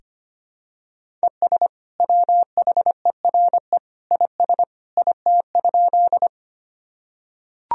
Morse code, heard as a keyed tone, about 25 wpm, 700 Hz. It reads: ES WHERE IS IT?
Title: Sentences from Top 200 Words and Abbreviations 25wpm